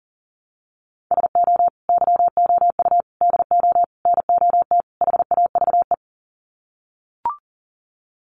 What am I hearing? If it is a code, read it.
Morse code, heard as a keyed tone, about 40 wpm, 700 Hz. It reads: SO YOU DO NOT HAVE
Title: Sets of 5 Words Encore 40wpm